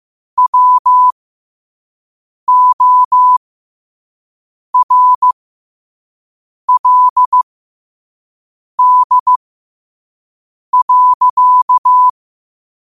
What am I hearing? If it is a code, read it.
Morse code, heard as a keyed tone, about 15 wpm, spaced as for 5 wpm, 1000 Hz. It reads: WORLD.
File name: qotdfiles / 5wpm